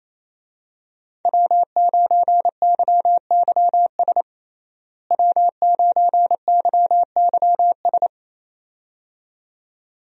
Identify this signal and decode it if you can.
Morse code, heard as a keyed tone, about 28 wpm, 700 Hz. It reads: W9YYS W9YYS
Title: USA Call Signs - 5 Characters - Repeated 2x - 28wpm